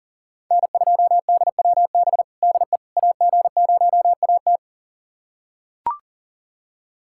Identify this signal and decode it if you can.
Morse code, heard as a keyed tone, about 40 wpm, 700 Hz. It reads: N2DWB DE AG0AT
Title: USA Callsign DE Callsign - 5 Characters - 40wpm